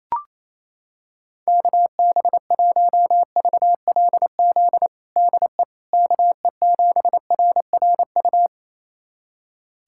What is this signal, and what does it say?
Morse code, heard as a keyed tone, about 27 wpm, 700 Hz. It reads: KB1VLZ DE KE7RRU
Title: USA Callsign DE Callsign 28wpm